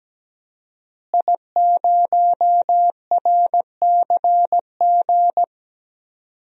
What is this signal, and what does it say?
Morse code, heard as a keyed tone, about 17 wpm, 700 Hz. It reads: I0RCG
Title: DXCC-100 Call Signs - 5 Characters - Repeated 2x - 17wpm